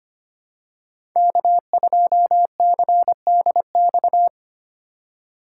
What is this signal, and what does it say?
Morse code, heard as a keyed tone, about 25 wpm, 700 Hz. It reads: K2CDX
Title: K1USN SST QSO Practice - 25wpm